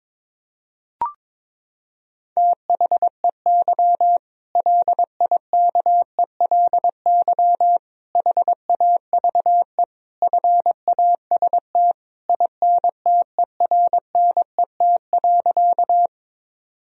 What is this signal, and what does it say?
Morse code, heard as a keyed tone, about 22 wpm, 700 Hz. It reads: THEY LIKELY HAVE FAST INTERNET.
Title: Sentences from Top 1000 Words - Review Part 4 - 22wpm